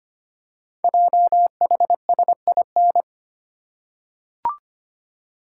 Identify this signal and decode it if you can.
Morse code, heard as a keyed tone, about 25 wpm, 700 Hz. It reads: JHSIN
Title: Code Groups - 5 Letters - 25wpm